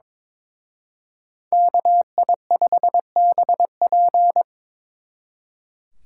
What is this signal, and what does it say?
Morse code, heard as a keyed tone, about 27 wpm, 700 Hz. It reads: KI5BP
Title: Speed Racing - USA 2x2 Call Signs 22wpm